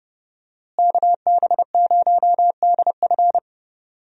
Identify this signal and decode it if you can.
Morse code, heard as a keyed tone, about 30 wpm, 700 Hz. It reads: KB0DF
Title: USA Call Signs - 5 Characters - Repeated 2x - 30wpm